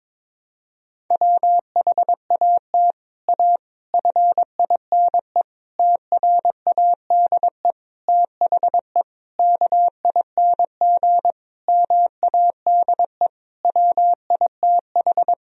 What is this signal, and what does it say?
Morse code, heard as a keyed tone, about 22 wpm, 700 Hz. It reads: WHAT A FINE TRADE THE KING MADE WITH
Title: Sentences from Top 1000 Words - Review Part 4 - 22wpm